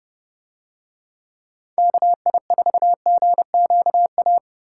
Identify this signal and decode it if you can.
Morse code, heard as a keyed tone, about 30 wpm, 700 Hz. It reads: KI4GQA